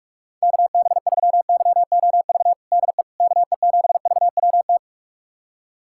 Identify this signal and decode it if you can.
Morse code, heard as a keyed tone, about 41 wpm, 700 Hz. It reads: KB3YOV DE KE7VWT
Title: USA Callsign DE Callsign - 6 Characters - 45wpm